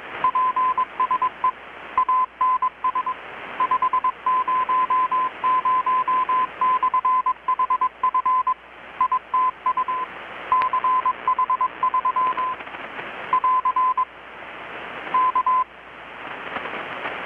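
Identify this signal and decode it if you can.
Morse code, heard as a keyed tone, about 21 wpm, 1020 Hz. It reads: PSE ANS 500/HF ITU CH3 <AR> K
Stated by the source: Morse from KPH's Night of Nights XX maritime radiotelegraphy event, received in Chiapas, Mexico about 3,800 km away on a Kenwood R-600 with a grounded 10 m long-wire antenna.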